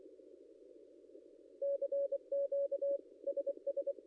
Morse code, heard as a keyed tone, about 24 wpm, 560 Hz. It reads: CQ SS
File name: contests / ss-cw-k5tr-op-n5rz -0700